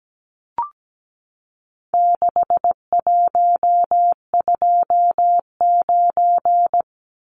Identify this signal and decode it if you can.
Morse code, heard as a keyed tone, about 17 wpm, 700 Hz. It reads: 6129
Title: Numbers - 4-Digit 17wpm